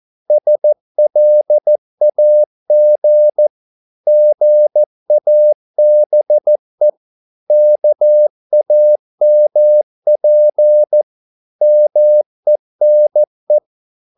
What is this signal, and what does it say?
Morse code, heard as a keyed tone, about 14 wpm, 600 Hz. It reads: SLAG GABE KAMP MENE
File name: Korte ord DK 14wpm